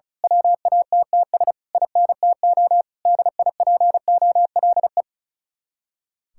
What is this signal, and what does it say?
Morse code, heard as a keyed tone, about 33 wpm, 700 Hz. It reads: WATTS INTO DIPOLE
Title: Speed-Racing Common QSO Phrases - 35wpm